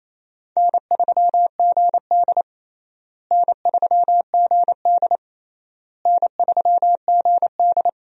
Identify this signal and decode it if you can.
Morse code, heard as a keyed tone, about 28 wpm, 700 Hz. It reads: N3GD N3GD N3GD
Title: USA Call Signs - 4 Characters - Repeated 3x - 28wpm